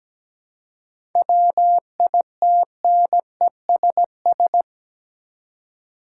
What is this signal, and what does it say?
Morse code, heard as a keyed tone, about 17 wpm, 700 Hz. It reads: WITNESS
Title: Seven Letter Words - 17wpm